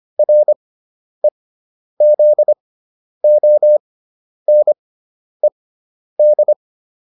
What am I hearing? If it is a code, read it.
Morse code, heard as a keyed tone, about 25 wpm, spaced as for 10 wpm, 600 Hz. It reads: REZONED